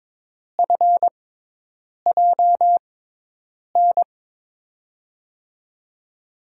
Morse code, heard as a keyed tone, about 22 wpm, spaced as for 8 wpm, 700 Hz. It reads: FJN
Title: USA Call Signs - 6x Character Spacing - 22wpm